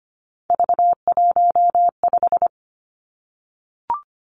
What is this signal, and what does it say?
Morse code, heard as a keyed tone, about 25 wpm, 700 Hz. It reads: V15
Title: Three Challenging Letter-Number Combinations 25wpm